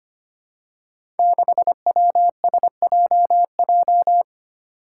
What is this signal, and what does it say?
Morse code, heard as a keyed tone, about 25 wpm, 700 Hz. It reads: 6WSJJ
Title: Five Challenging Letter-Number Combinations 25wpm